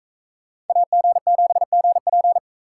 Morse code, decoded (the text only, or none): AG7GP